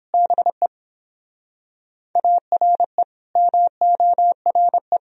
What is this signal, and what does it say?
Morse code, heard as a keyed tone, about 31 wpm, 700 Hz. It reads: BE ARE MORE